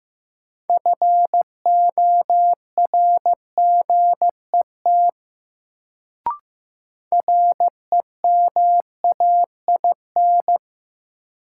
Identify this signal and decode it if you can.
Morse code, heard as a keyed tone, about 15 wpm, 700 Hz. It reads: FORGET REMAIN
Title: Sentences from Top 1200 Words - New Word Drill - 15wpm